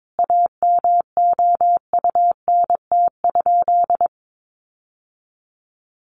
Morse code, heard as a keyed tone, about 22 wpm, 700 Hz. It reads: AMOUNT?
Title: Sentences from Top 700 Words 22wpm